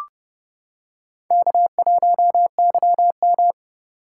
Morse code, heard as a keyed tone, about 28 wpm, 700 Hz. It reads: K1YM